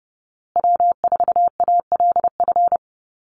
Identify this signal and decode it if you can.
Morse code, heard as a keyed tone, about 30 wpm, 700 Hz. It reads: W4ALF
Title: Call Signs Repeated 2x - 30wpm